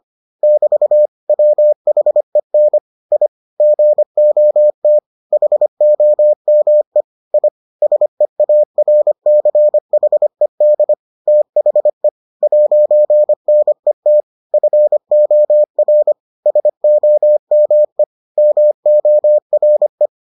Morse code, heard as a keyed tone, about 25 wpm, 600 Hz. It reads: <BT> WHEN I GOT HOME I SEARCHED THE 'NET FOR SOME MORE